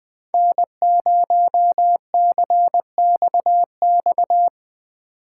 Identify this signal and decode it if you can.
Morse code, heard as a keyed tone, about 22 wpm, 700 Hz. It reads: N0CXX
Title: Speed Racing - USA 1x3 Call Signs 20wpm